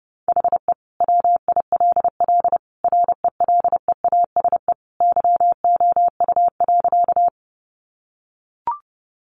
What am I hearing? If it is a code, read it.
Morse code, heard as a keyed tone, about 30 wpm, 700 Hz. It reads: HE WILL RELEASE YOU.
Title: Sentences from Top 1100 Words 30wpm